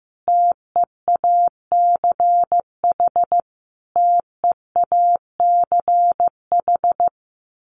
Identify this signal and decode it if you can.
Morse code, heard as a keyed tone, about 15 wpm, 700 Hz. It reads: TEACH TEACH
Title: Top 500 Words Repeated 15wpm